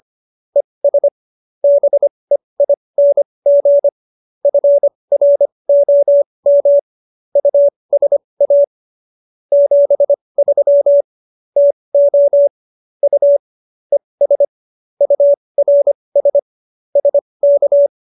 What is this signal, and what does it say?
Morse code, heard as a keyed tone, about 25 wpm, spaced as for 20 wpm, 570 Hz. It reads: ES BEING FROM USA 73 TO U ES URS SK